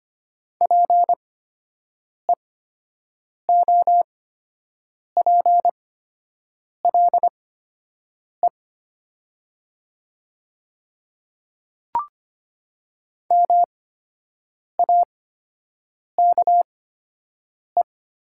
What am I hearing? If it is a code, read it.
Morse code, heard as a keyed tone, about 25 wpm, spaced as for 7 wpm, 700 Hz. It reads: PEOPLE MAKE
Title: 8x Character Spacing - Top 100 Words - 25wpm